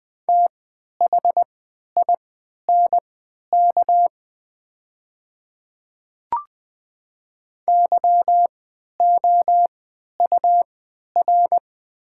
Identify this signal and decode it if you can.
Morse code, heard as a keyed tone, about 20 wpm, spaced as for 11 wpm, 700 Hz. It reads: THINK YOUR
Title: 3x Character Spacing - Top 200 Words - 20wpm